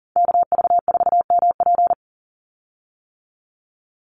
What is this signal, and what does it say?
Morse code, heard as a keyed tone, about 40 wpm, 700 Hz. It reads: KV4MP
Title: USA Call Signs 40wpm